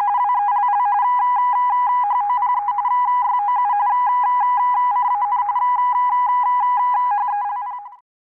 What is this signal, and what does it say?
RTTY, about 45.45 baud. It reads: WELL HOPE YOU CAN HELP